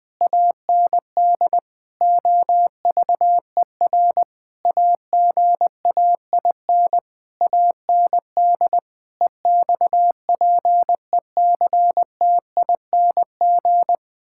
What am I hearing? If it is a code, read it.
Morse code, heard as a keyed tone, about 21 wpm, 700 Hz. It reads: AND OVER AGAIN AND EXPECTING
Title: Sentences with 9-Word Phrases - Spoken Beginning - 20wpm